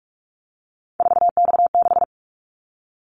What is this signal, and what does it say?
Morse code, heard as a keyed tone, about 44 wpm, 700 Hz. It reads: 4X6
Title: Callsign Prefixes for Top 50 DXCC - 45wpm